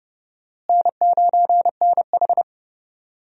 Morse code, heard as a keyed tone, about 30 wpm, 700 Hz. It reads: N9NH
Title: USA Call Signs - 4 Characters - Repeated 3x - 30wpm